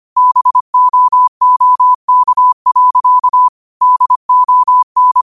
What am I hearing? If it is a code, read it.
Morse code, heard as a keyed tone, about 25 wpm, 1000 Hz. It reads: BOOK. DON
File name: 25wpm